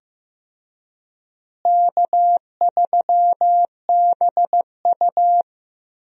Morse code, heard as a keyed tone, about 15 wpm, 700 Hz. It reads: K3BU
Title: USA Call Signs - 4 Characters - Repeated 3x - 15wpm